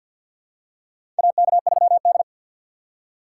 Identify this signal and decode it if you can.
Morse code, heard as a keyed tone, about 47 wpm, 700 Hz. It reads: AK3D
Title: USA Call Signs - 4 Characters - 3x Character Spacing - 50wpm